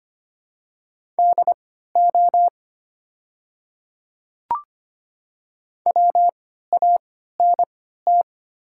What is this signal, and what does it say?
Morse code, heard as a keyed tone, about 25 wpm, spaced as for 14 wpm, 700 Hz. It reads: DO WANT
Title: 3x Character Spacing - Top 200 Words - 25wpm